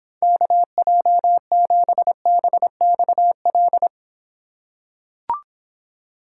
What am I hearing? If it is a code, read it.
Morse code, heard as a keyed tone, about 24 wpm, 700 Hz. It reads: KJ7BXL